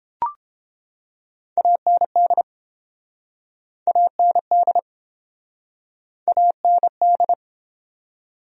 Morse code, heard as a keyed tone, about 30 wpm, 700 Hz. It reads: AND AND AND